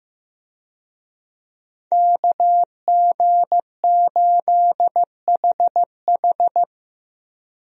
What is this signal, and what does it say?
Morse code, heard as a keyed tone, about 15 wpm, 700 Hz. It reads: KG8HH
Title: USA Call Signs - 5 Characters - 15wpm